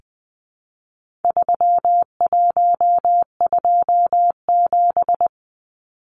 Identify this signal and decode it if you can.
Morse code, heard as a keyed tone, about 20 wpm, 700 Hz. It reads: 3127
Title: Numbers - 4-Digit 20wpm